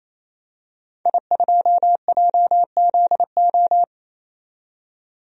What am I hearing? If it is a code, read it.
Morse code, heard as a keyed tone, about 27 wpm, 700 Hz. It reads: I2JZO
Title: DXCC-100 Call Signs - 5 Characters - 28wpm